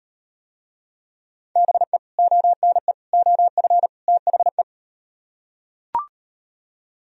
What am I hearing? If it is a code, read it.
Morse code, heard as a keyed tone, about 35 wpm, 700 Hz. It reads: BE ONE OF THE